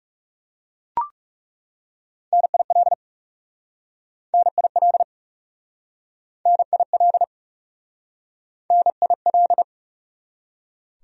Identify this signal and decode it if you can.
Morse code, heard as a keyed tone, about 38 wpm, 700 Hz. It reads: NIL NIL NIL NIL